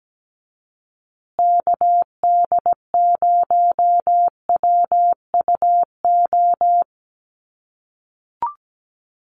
Morse code, heard as a keyed tone, about 17 wpm, 700 Hz. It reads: KD0WUO
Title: USA Call Signs Repeated 3x 17wpm